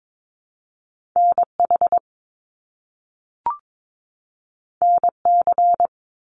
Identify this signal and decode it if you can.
Morse code, heard as a keyed tone, about 22 wpm, 700 Hz. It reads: NH NC
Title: US States 22wpm